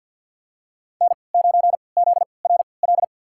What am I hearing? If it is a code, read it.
Morse code, heard as a keyed tone, about 51 wpm, spaced as for 28 wpm, 700 Hz. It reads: N9ZRL
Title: USA Call Signs - 5 Characters - 3x Character Spacing - 50wpm